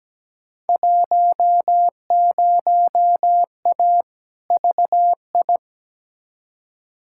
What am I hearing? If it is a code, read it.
Morse code, heard as a keyed tone, about 17 wpm, 700 Hz. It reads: 10A VI
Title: Field Day Exchanges Encore - 17wpm